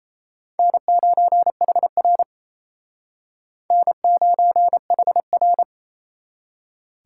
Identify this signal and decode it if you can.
Morse code, heard as a keyed tone, about 38 wpm, 700 Hz. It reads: N9HR N9HR